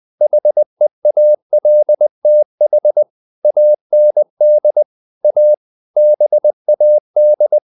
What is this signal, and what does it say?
Morse code, heard as a keyed tone, about 20 wpm, 600 Hz. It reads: HEALTH AND A BAD
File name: CW / Practice / EnglishQuotations0320